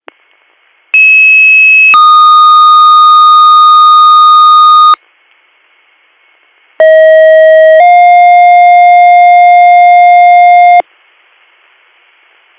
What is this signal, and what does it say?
Two-tone page, heard as a one-second tone, then a three-second tone, about 2600 then 1230 Hz, about 640 then 710 Hz. It is heard from 0.9 to 10.8 s.